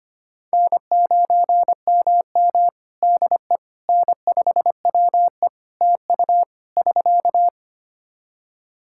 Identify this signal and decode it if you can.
Morse code, heard as a keyed tone, about 25 wpm, 700 Hz. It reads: N9MM DE N5WE TU <SK>
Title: Practice QSO - Exchange 4 - Ending - 25wpm